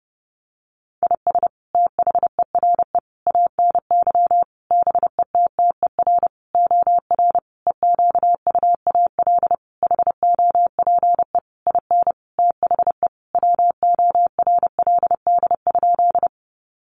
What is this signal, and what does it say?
Morse code, heard as a keyed tone, about 30 wpm, 700 Hz. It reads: IS THERE ANY BETTER OR EQUAL HOPE IN THE WORLD?
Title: Sentences from Top 1400 Words 30wpm